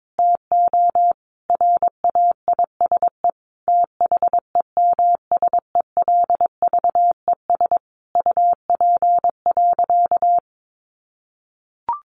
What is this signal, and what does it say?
Morse code, heard as a keyed tone, about 22 wpm, 700 Hz. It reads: TO RAISE THEMSELVES UP.
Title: Sentences from Top 1400 Words 22wpm